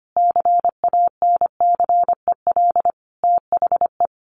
Morse code, heard as a keyed tone, about 25 wpm, 700 Hz. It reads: CANCEL THE
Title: Sentences from Top 1200 Words 25wpm